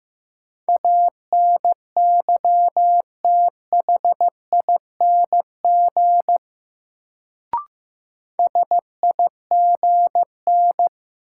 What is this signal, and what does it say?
Morse code, heard as a keyed tone, about 15 wpm, 700 Hz. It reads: ANYTHING SIGN
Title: Sentences from Top 1000 Words - New Word Drill - 15wpm